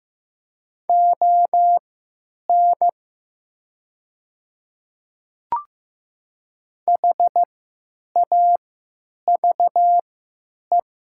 Morse code, heard as a keyed tone, about 15 wpm, spaced as for 9 wpm, 700 Hz. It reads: ON HAVE
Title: 3x Character Spacing - Top 300 Words - 15wpm